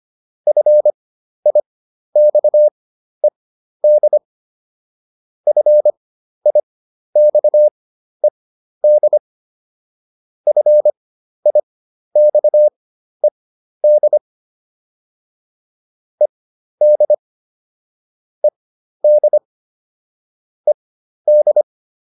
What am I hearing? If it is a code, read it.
Morse code, heard as a keyed tone, about 25 wpm, spaced as for 12 wpm, 600 Hz. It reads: FIXED FIXED FIXED ED ED ED